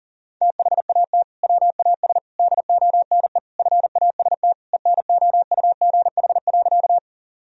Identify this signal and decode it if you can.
Morse code, heard as a keyed tone, about 40 wpm, 700 Hz. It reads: THAT WAS DONE FAST ENOUGH.
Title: Sentences from Top 1000 Words - Review Part 2 - 40wpm